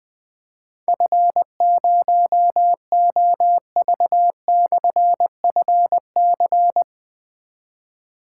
Morse code, heard as a keyed tone, about 20 wpm, 700 Hz. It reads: F0OV/FC
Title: Difficult Call Signs 20wpm